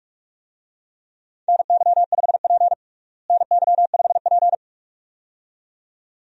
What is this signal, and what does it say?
Morse code, heard as a keyed tone, about 42 wpm, 700 Hz. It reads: NY5P NY5P
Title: USA Call Signs - 4 Characters - Repeated 2x - 45wpm